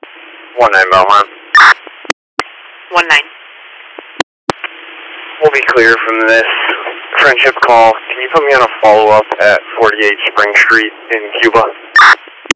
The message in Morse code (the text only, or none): none